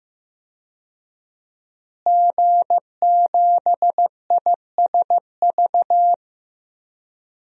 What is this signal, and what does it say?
Morse code, heard as a keyed tone, about 15 wpm, 700 Hz. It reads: G7ISV
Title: DXCC-100 Call Signs - 5 Characters - Repeated 2x - 15wpm